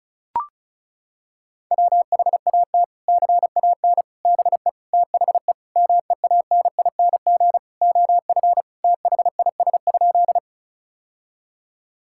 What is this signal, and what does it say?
Morse code, heard as a keyed tone, about 36 wpm, 700 Hz. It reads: WHAT CAN BE THE MEANING OF THIS?
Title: Sentences from Top 1000 Words - Review Part 6 - 35wpm